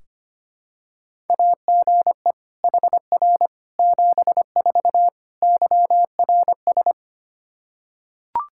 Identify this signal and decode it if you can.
Morse code, heard as a keyed tone, about 25 wpm, 700 Hz. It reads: AGE HR 74 YRS
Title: Practice QSO - Exchange 3 - More Info - 25wpm